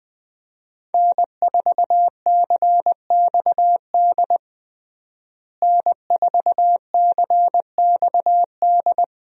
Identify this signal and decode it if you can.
Morse code, heard as a keyed tone, about 20 wpm, 700 Hz. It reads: N4CXD N4CXD
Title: USA Call Signs - 5 Characters - Repeated 2x - 20wpm